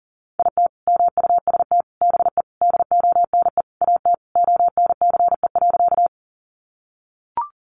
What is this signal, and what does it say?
Morse code, heard as a keyed tone, about 40 wpm, 700 Hz. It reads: IT MUST BE DONE AT ONCE.
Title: Sentences from Top 500 Words 40wpm